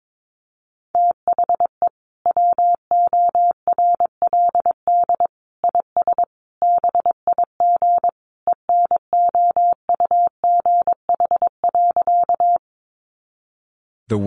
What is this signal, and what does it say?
Morse code, heard as a keyed tone, about 22 wpm, 700 Hz. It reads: THE WORLD IS BIG ENOUGH.
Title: Sentences from Top 400 Words 22wpm